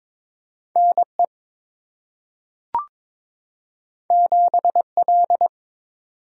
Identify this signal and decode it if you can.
Morse code, heard as a keyed tone, about 22 wpm, 700 Hz. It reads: NE 7L
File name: practice / 22wpm / Two Letter-Number Combinations - 22wpm